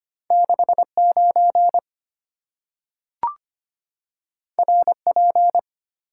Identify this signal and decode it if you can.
Morse code, heard as a keyed tone, about 25 wpm, 700 Hz. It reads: 6 9 R P
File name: Two Letter-Number Combinations - 25wpm